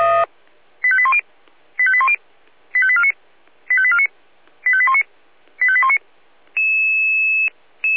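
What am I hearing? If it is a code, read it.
Five-tone selective call (ZVEI), heard as quick runs of five tones, tones of about 70 ms. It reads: 85810 85810 85849 85849 85819 85819
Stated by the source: ZVEI five-tone call 85870 received on an SDR.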